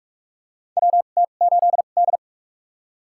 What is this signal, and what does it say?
Morse code, heard as a keyed tone, about 47 wpm, spaced as for 33 wpm, 700 Hz. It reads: WT8D
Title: USA Call Signs - 4 Characters - 2x Character Spacing - 45wpm